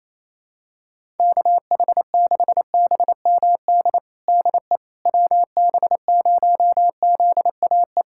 Morse code, heard as a keyed tone, about 28 wpm, 700 Hz. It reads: KH6BMD DE WB0ZAE
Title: USA Callsign DE Callsign - 6 Characters - 28wpm